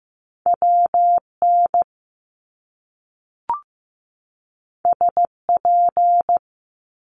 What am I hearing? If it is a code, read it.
Morse code, heard as a keyed tone, about 15 wpm, 700 Hz. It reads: WN SP
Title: Top Two Letter Combinations in Words 15wpm